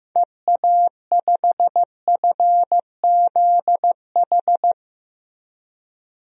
Morse code, heard as a keyed tone, about 15 wpm, 700 Hz. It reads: EA5FZH
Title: DXCC-100 Call Signs - 6 Characters - Repeated 2x - 15wpm